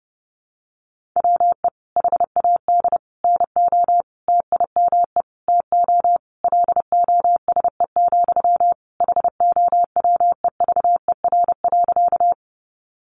Morse code, heard as a keyed tone, about 30 wpm, 700 Hz. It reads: WE HAD NO TIME TO LOSE, HOWEVER.
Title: Sentences from Top 1100 Words 30wpm